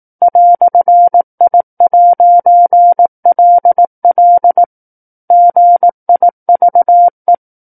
Morse code, heard as a keyed tone, about 18 wpm, 700 Hz. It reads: "I'LL GIVE
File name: 18 WPM